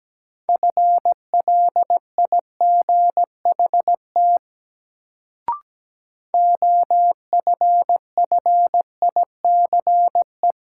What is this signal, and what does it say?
Morse code, heard as a keyed tone, about 17 wpm, 700 Hz. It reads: FLIGHT OFFICE
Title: Sentences from Top 1000 Words - New Word Drill - 17wpm